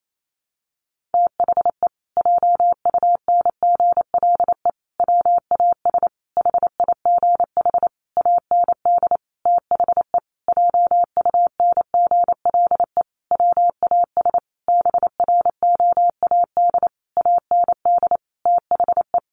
Morse code, heard as a keyed tone, about 28 wpm, 700 Hz. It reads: THE JUNGLE WAS HIGH AND THE JUNGLE WAS BROAD AND THE
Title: A Sound Of Thunder 28wpm